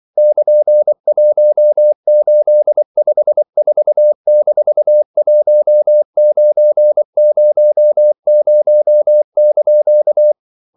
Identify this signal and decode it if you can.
Morse code, heard as a keyed tone, about 24 wpm, 600 Hz. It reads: <KN>1854-1900)
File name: CW / Practice / EnglishQuotations0324